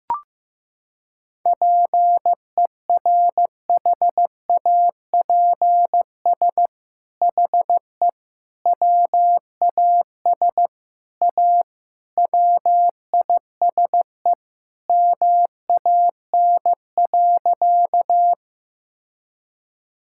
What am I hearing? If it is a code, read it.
Morse code, heard as a keyed tone, about 15 wpm, 700 Hz. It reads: PERHAPS HE WAS A WISE MAN.
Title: Sentences from Top 1600 Words 15wpm